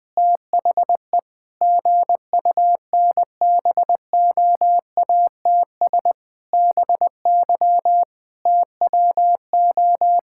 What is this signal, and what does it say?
Morse code, heard as a keyed tone, about 20 wpm, 700 Hz. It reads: THE GUNBOATS BY TWO
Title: Pangram Challenge v2 - 20wpm